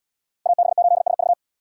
Morse code, heard as a keyed tone, about 77 wpm, 700 Hz. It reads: AB8SV